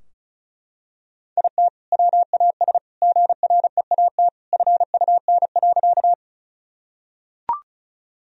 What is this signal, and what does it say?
Morse code, heard as a keyed tone, about 34 wpm, 700 Hz. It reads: IT WAS GREAT FUN.
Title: Sentences from Top 1000 Words - Review Part 4 - 35wpm